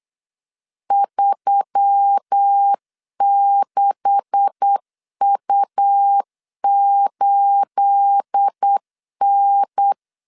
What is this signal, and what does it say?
Morse code, heard as a keyed tone, about 8 wpm, 800 Hz. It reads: 36U8N
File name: LEZIONI / 9 286 8wpm 3425